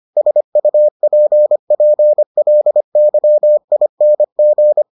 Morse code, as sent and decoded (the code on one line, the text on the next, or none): ... ..- .--. .--. .-.. -.-- .. -. --.
SUPPLYING